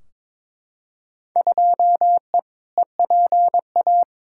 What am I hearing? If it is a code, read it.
Morse code, heard as a keyed tone, about 22 wpm, 700 Hz. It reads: 2E EPA
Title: Field Day Exchanges Encore - 22wpm